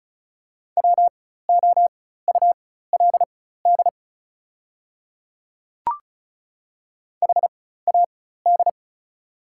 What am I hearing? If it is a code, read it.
Morse code, heard as a keyed tone, about 36 wpm, spaced as for 16 wpm, 690 Hz. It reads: WOULD HAD
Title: 4x Character Spacing - Top 100 Words - 35wpm